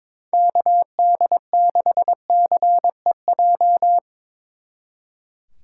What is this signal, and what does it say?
Morse code, heard as a keyed tone, about 25 wpm, 700 Hz. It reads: KD6CEJ
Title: Speed Racing - USA 2x3 Call Signs 22wpm